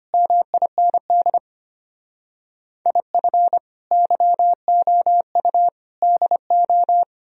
Morse code, heard as a keyed tone, about 33 wpm, 700 Hz. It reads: MIND IF YOU DO